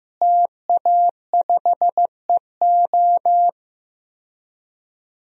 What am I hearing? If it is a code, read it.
Morse code, heard as a keyed tone, about 15 wpm, 700 Hz. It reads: TA5EO
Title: DXCC-100 Call Signs - 5 Characters - Repeated 2x - 15wpm